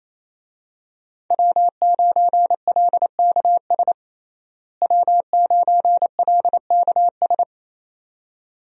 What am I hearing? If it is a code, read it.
Morse code, heard as a keyed tone, about 28 wpm, 700 Hz. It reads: W9LKS W9LKS
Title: USA Call Signs - 5 Characters - Repeated 2x - 28wpm